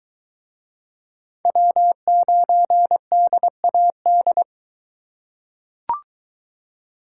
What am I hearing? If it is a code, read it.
Morse code, heard as a keyed tone, about 20 wpm, 700 Hz. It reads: W9DAD